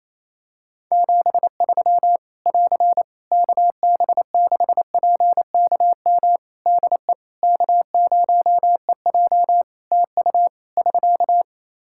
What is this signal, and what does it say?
Morse code, heard as a keyed tone, about 28 wpm, 700 Hz. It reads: 73 <AR> KB6PKM DE K0EJ TU <SK>
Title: Practice QSO - Exchange 4 - Ending - 28wpm